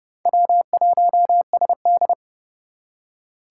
Morse code, heard as a keyed tone, about 32 wpm, 700 Hz. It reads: W1SD
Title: USA Call Signs - 4 Characters - 3x Character Spacing - 30wpm